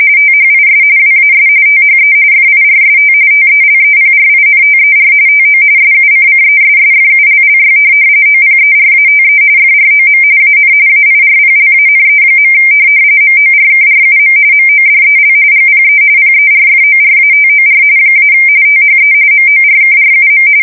RTTY, about 45.45 baud. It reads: THIS WAS THE FIRST TELEPRINTER TO SUCCESSFULLY OPERATE FROM AN AIRPLANE. MORKRUM MODEL GPE PERFORATOR - THE MORKRUM
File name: fsk170 / teletype-corp